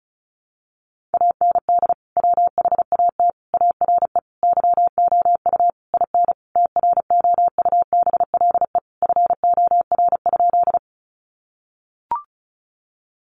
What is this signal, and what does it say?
Morse code, heard as a keyed tone, about 34 wpm, 700 Hz. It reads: AND WHAT ARE YOU IN TROUBLE FOR?
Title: Sentences from Top 1200 Words 35wpm